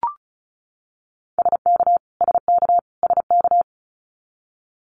Morse code, heard as a keyed tone, about 34 wpm, 700 Hz. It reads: SK SK SK
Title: Canadian Provinces and Territories - Repeated 3x - 35wpm